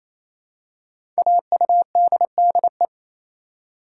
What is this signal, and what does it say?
Morse code, heard as a keyed tone, about 28 wpm, 700 Hz. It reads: AUDDE
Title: Code Groups - 5 Letters - 28wpm